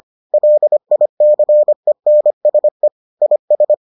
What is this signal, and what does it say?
Morse code, heard as a keyed tone, about 25 wpm, 600 Hz. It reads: LICENSE IS